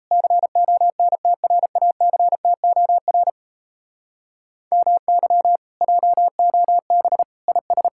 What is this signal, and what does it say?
Morse code, heard as a keyed tone, about 38 wpm, 700 Hz. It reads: CONTRACTOR MY JOB IS